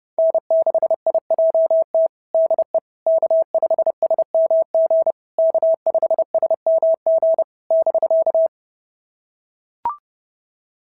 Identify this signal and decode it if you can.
Morse code, heard as a keyed tone, about 28 wpm, 650 Hz. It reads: N6IJT DE K5SMG K5SMG <BK>
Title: SOTA QSO Practice 30wpm